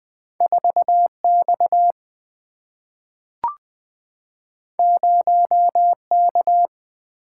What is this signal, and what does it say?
Morse code, heard as a keyed tone, about 20 wpm, 700 Hz. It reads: 4 X 0 K
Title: Single-Letter Single-Number 20wpm